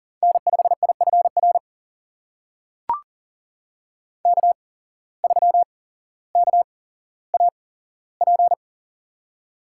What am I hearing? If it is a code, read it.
Morse code, heard as a keyed tone, about 40 wpm, spaced as for 17 wpm, 700 Hz. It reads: N5IFR K3KAP
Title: USA Call Signs - 5 Characters - 8x Character Spacing - 40wpm